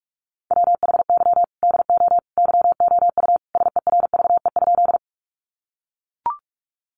Morse code, heard as a keyed tone, about 44 wpm, 700 Hz. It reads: WHY DO YOU SERVE?
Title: Sentences from Top 400 Words 45wpm